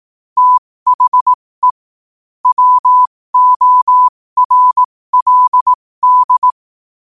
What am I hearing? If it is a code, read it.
Morse code, heard as a keyed tone, about 18 wpm, spaced as for 15 wpm, 1000 Hz. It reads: THE WORLD
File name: qotdfiles / 15wpm